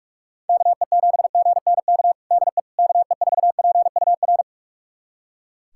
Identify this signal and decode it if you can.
Morse code, heard as a keyed tone, about 41 wpm, 700 Hz. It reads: KE7GNK DE KE4PUR
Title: USA Callsign DE Callsign - 6 Characters - 45wpm